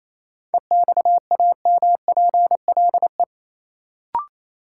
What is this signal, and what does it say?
Morse code, heard as a keyed tone, about 28 wpm, 700 Hz. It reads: EXAMPLE
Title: Sets of 2 Words - New Word Drill - 28wpm